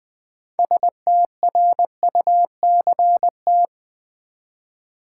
Morse code, heard as a keyed tone, about 20 wpm, 700 Hz. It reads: STRUCT
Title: Build A Word - 3 Letters At A Time - 9-Letter Words - 20wpm